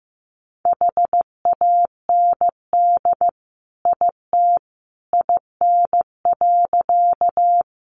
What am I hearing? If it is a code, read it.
Morse code, heard as a keyed tone, about 15 wpm, 700 Hz. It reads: HAND IT IN.
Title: Sentences from Top 200 Words 15wpm